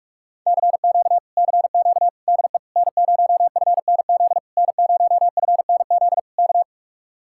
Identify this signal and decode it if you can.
Morse code, heard as a keyed tone, about 44 wpm, 700 Hz. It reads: CQ CQ DE N0FNZ N0FNZ K